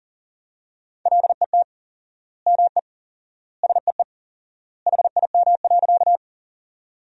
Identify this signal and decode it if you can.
Morse code, heard as a keyed tone, about 40 wpm, 700 Hz. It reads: LET ME SEE HIM.
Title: 4x Word-Spacing - Sentences from Top 300 Words - 40wpm